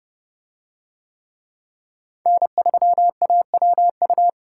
Morse code, heard as a keyed tone, about 28 wpm, 700 Hz. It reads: N3AWU